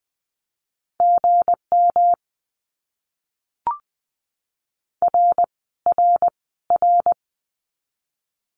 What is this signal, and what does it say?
Morse code, heard as a keyed tone, about 20 wpm, 700 Hz. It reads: GM R R R
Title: Abbreviations Repeated 3x - 20wpm